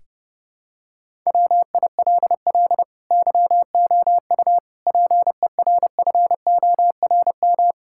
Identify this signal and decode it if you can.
Morse code, heard as a keyed tone, about 30 wpm, 700 Hz. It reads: WILL YOU PERFORM
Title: Sentences from Top 1000 Words - Review Part 5 - 30wpm